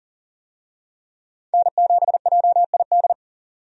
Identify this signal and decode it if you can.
Morse code, heard as a keyed tone, about 40 wpm, 700 Hz. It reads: N7JID
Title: USA Call Signs - 5 Characters - 40wpm